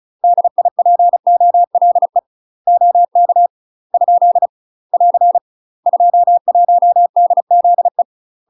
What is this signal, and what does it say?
Morse code, heard as a keyed tone, about 34 wpm, 700 Hz. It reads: DIPOLE OK ? <AR> 21DZE
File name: UK MASTER QSOs TRAINING / 35WPM / UK MASTER QSOs 0039